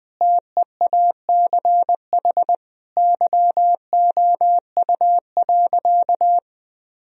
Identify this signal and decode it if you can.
Morse code, heard as a keyed tone, about 20 wpm, 700 Hz. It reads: TEACH YOU.
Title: Sentences from Top 1000 Words - Review Part 2 - 20wpm